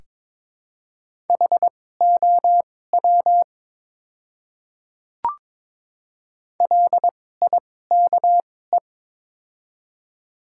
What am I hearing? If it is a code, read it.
Morse code, heard as a keyed tone, about 22 wpm, spaced as for 16 wpm, 700 Hz. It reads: HOW LIKE
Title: 2x Character Spacing - Top 200 Words - 22wpm